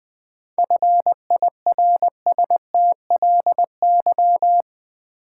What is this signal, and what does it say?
Morse code, heard as a keyed tone, about 20 wpm, 700 Hz. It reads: FIRSTLY
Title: 7 Letter Words - First 3 Letters Given - 20wpm